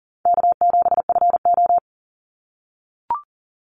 Morse code, heard as a keyed tone, about 40 wpm, 700 Hz. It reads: K7FO
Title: NA SOTA Chaser Callsigns 40wpm